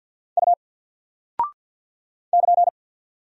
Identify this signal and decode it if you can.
Morse code, heard as a keyed tone, about 48 wpm, 700 Hz. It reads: U <KN>
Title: Single Character (Letter:Number:Punctuation:Prosign) Alt 50wpm